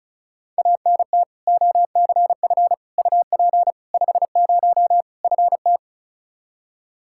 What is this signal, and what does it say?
Morse code, heard as a keyed tone, about 34 wpm, 700 Hz. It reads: ANT OCF UP 50 FT
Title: K1USN SST QSO Practice 35wpm